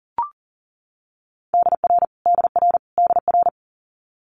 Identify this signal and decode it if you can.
Morse code, heard as a keyed tone, about 38 wpm, 700 Hz. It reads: DR DR DR
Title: Multi-letter Phonemes Repeated 3x 40wpm